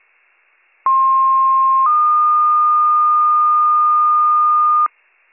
Two-tone page, heard as a one-second tone, then a three-second tone, about 1060 then 1220 Hz, with noise. It is heard from 0.9 to 4.9 s.